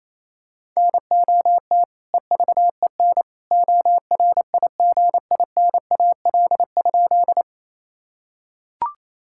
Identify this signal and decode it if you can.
Morse code, heard as a keyed tone, about 27 wpm, 700 Hz. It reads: NOT EVEN ORIGINAL?
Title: Sentences from Top 1000 Words - Review Part 6 - 28wpm